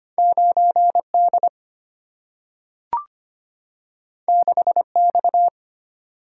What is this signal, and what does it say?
Morse code, heard as a keyed tone, about 25 wpm, 690 Hz. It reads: 9D 6X
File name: Two Letter-Number Combinations - 25wpm